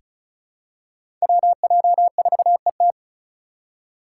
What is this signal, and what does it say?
Morse code, heard as a keyed tone, about 36 wpm, 690 Hz. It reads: WJ4ET
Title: USA Call Signs - 3x Character Spacing - 35wpm